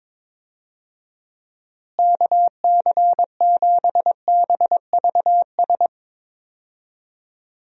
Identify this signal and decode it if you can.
Morse code, heard as a keyed tone, about 22 wpm, 700 Hz. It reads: KC7BVS
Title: USA Call Signs - 6 Characters - Repeated 2x - 22wpm